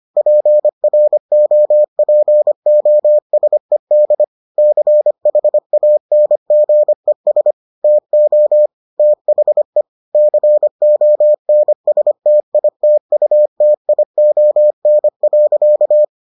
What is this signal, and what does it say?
Morse code, heard as a keyed tone, about 25 wpm, 600 Hz. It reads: PROPOSED CHANGES TO THE CONSTITUTION.